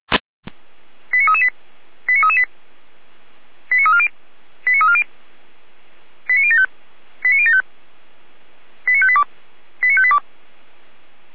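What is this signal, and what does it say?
Five-tone selective call (ZVEI), heard as quick runs of five tones, tones of about 80 ms. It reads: 89338 89338 89340 89340 89075 89075 89582 89582
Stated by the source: Received on an SDR; ZVEI five-tone code 89336.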